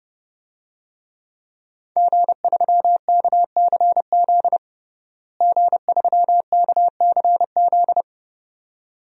Morse code, heard as a keyed tone, about 32 wpm, 700 Hz. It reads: G3KCZ G3KCZ
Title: DXCC-100 Call Signs - 5 Characters - Repeated 2x - 30wpm